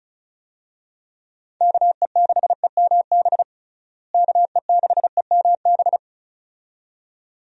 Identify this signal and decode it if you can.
Morse code, heard as a keyed tone, about 36 wpm, 700 Hz. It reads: KE6EMB KE6EMB
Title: USA Call Signs - 6 Characters - Repeated 2x - 35wpm